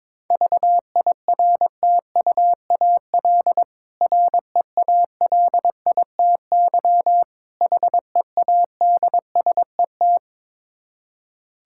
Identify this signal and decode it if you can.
Morse code, heard as a keyed tone, about 21 wpm, 690 Hz. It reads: VIRTUAL REALITY HEADSET
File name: Difficult Words with Hints - 22wpm